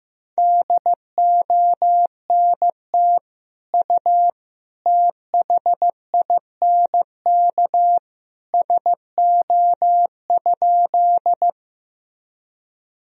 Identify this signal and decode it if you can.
Morse code, heard as a keyed tone, about 15 wpm, 700 Hz. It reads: DONT U THINK SO?
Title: Sentences from Top 300 Words and Abbreviations 15wpm